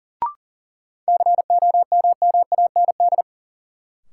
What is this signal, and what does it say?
Morse code, heard as a keyed tone, about 40 wpm, 700 Hz. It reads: COMMAND
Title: Sentences from Top 1600 Words - New Word Drill - 40wpm